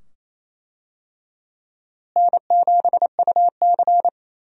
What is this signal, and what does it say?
Morse code, heard as a keyed tone, about 28 wpm, 700 Hz. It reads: N7UC
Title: USA Call Signs - 4 Characters - Repeated 2x - 28wpm